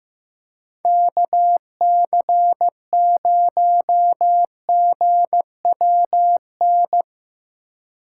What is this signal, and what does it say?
Morse code, heard as a keyed tone, about 17 wpm, 700 Hz. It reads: KC0GWN
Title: Speed Racing - USA 2x3 Call Signs 15wpm